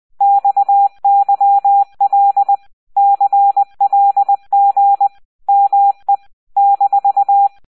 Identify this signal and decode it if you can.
Morse code, heard as a keyed tone, about 20 wpm, 800 Hz. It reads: XYL CLG ME -